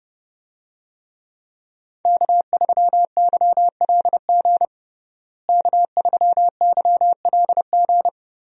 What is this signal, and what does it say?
Morse code, heard as a keyed tone, about 30 wpm, 700 Hz. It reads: K3YLG K3YLG
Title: USA Call Signs - 5 Characters - Repeated 2x - 30wpm